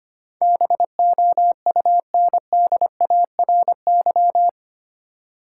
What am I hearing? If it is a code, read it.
Morse code, heard as a keyed tone, about 25 wpm, 700 Hz. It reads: BOUNDARY
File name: 8 Letter Words - First 3 Letters Given - 25wpm